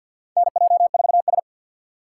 Morse code, heard as a keyed tone, about 51 wpm, 700 Hz. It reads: NJ4S
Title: USA Call Signs - 4 Characters - 2x Character Spacing - 50wpm